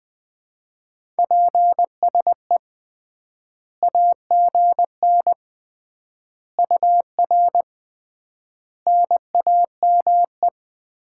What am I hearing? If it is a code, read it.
Morse code, heard as a keyed tone, about 20 wpm, 700 Hz. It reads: PSE AGN UR NAME
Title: Common QSO Phrases - 3x Word-Spacing - 20wpm